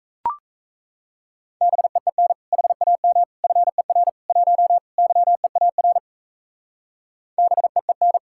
Morse code, heard as a keyed tone, about 32 wpm, 700 Hz. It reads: BEEN HAM FER 1 YEAR BEEN
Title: Speed-Racing Common QSO Phrases - 28wpm